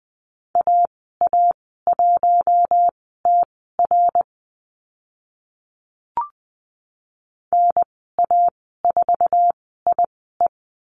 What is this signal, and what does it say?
Morse code, heard as a keyed tone, about 20 wpm, spaced as for 15 wpm, 700 Hz. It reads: AA1TR NA4IE
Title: 062 Lesson 11g - Letter h - Call Signs 20wpm